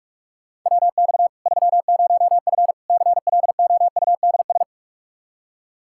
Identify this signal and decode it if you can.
Morse code, heard as a keyed tone, about 42 wpm, 700 Hz. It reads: WX 30F CLOUDS
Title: Practice QSO - Exchange 2 - Info - 45wpm